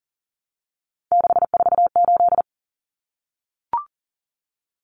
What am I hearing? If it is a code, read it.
Morse code, heard as a keyed tone, about 40 wpm, 700 Hz. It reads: 648
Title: Numbers - 3-Digit 40wpm